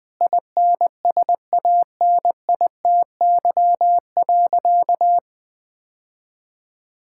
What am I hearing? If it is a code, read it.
Morse code, heard as a keyed tone, about 20 wpm, 700 Hz. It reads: INSANITY.
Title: Sentences with 9-Word Phrases - 20wpm